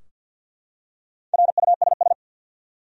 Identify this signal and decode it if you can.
Morse code, heard as a keyed tone, about 52 wpm, 700 Hz. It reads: RUSS
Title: USA Common Names - Max 4 Letters - 50wpm